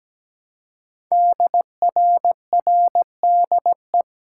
Morse code, heard as a keyed tone, about 17 wpm, 700 Hz. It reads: DRRDE
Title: Code Groups - 5 Letters - 17wpm